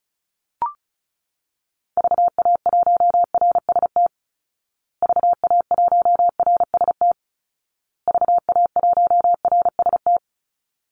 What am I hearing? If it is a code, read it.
Morse code, heard as a keyed tone, about 34 wpm, 700 Hz. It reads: VA1RST VA1RST VA1RST
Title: Call Signs Repeated 2x - 35wpm